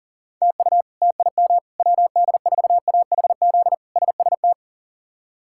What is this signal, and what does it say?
Morse code, heard as a keyed tone, about 40 wpm, 700 Hz. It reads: TU TIM WD4AHZ SST
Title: K1USN SST QSO Practice - 40wpm